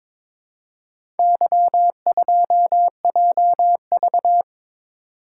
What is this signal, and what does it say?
Morse code, heard as a keyed tone, about 22 wpm, 700 Hz. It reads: Y2JV
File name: practice / 22wpm / Four Letter-Number Combinations - 22wpm